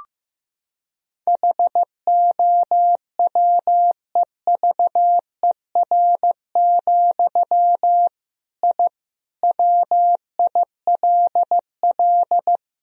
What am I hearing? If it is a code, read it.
Morse code, heard as a keyed tone, about 16 wpm, 700 Hz. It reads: HOWEVER, I WILL
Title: Sentences from Top 1000 Words - Review Part 1 - 15wpm